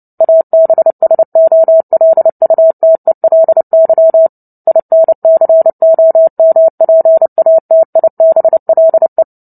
Morse code, heard as a keyed tone, about 30 wpm, 650 Hz. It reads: ABSOLUTELY INCOMPATIBLE
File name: Book Two / 03-Chapter 3 - 29 WPM